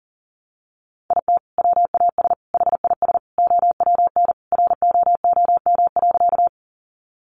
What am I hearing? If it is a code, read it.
Morse code, heard as a keyed tone, about 40 wpm, 700 Hz. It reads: IT WAS HIS OWN ROOM.
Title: Sentences from Top 300 Words 40wpm